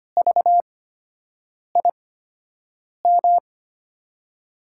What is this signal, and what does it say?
Morse code, heard as a keyed tone, about 25 wpm, spaced as for 7 wpm, 700 Hz. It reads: VIM
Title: USA Call Signs - 5 Characters - 8x Character Spacing - 25wpm